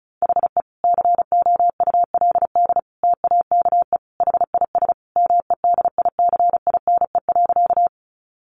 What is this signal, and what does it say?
Morse code, heard as a keyed tone, about 34 wpm, 700 Hz. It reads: HE COULD TAKE HIS MEDICINE.
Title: Sentences from Top 900 Words 35wpm